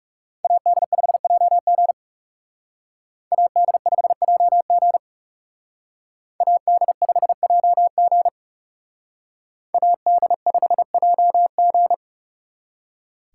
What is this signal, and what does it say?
Morse code, heard as a keyed tone, about 35 wpm, 700 Hz. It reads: AD5JG AD5JG AD5JG AD5JG
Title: Speed Racing - USA 2x2 Call Signs 30wpm